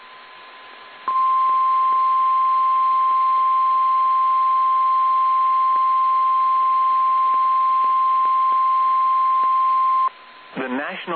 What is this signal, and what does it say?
Emergency Alert System (SAME) tones (the 1050 Hz attention tone), with speech. It is heard from 1.1 to 10.1 s.
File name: eas / KSDZKDJL / KSDZKDJL